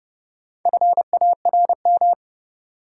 Morse code, heard as a keyed tone, about 29 wpm, 700 Hz. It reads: FARM
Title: Sentences from Top 1000 Words - Review Part 2 - New Word Drill - 30wpm